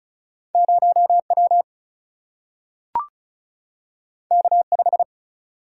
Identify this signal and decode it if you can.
Morse code, heard as a keyed tone, about 36 wpm, 700 Hz. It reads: 0 W K 5
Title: Single-Letter Single-Number 35wpm